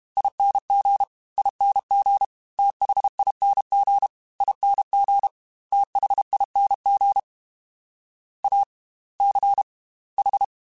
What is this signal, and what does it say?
Morse code, heard as a keyed tone, about 32 wpm, spaced as for 27 wpm, 800 Hz. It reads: ING ING THING ING THING ACH
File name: Slow ing @30